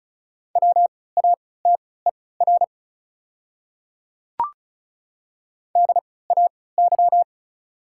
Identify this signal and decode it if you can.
Morse code, heard as a keyed tone, about 34 wpm, spaced as for 20 wpm, 690 Hz. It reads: WATER DAY
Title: 3x Character Spacing - Top 100 Words - 35wpm